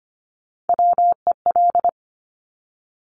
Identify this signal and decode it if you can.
Morse code, heard as a keyed tone, about 25 wpm, 700 Hz. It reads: WEL
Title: Top Three Letter Combinations in Words 25wpm